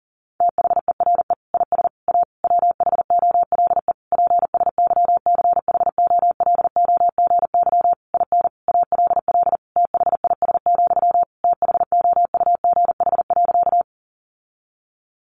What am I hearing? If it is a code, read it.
Morse code, heard as a keyed tone, about 40 wpm, 700 Hz. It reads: THERE IS A WHOLE PSYCHOLOGY IN ALL THIS, THOUGH.
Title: Sentences from Top 1400 Words 40wpm